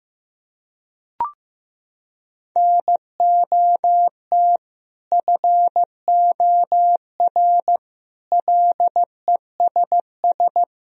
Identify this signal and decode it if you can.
Morse code, heard as a keyed tone, about 16 wpm, 700 Hz. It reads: NOT FOR LESS
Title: Sentences from Top 1000 Words - Review Part 3 - 15wpm